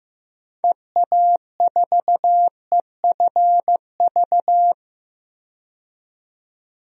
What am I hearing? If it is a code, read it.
Morse code, heard as a keyed tone, about 15 wpm, 700 Hz. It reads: EA4EFV
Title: DXCC-100 Call Signs - 6 Characters - 15wpm